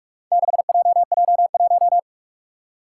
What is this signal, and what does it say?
Morse code, heard as a keyed tone, about 44 wpm, 700 Hz. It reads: 6JJ1
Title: Four Challenging Letter-Number Combinations 45wpm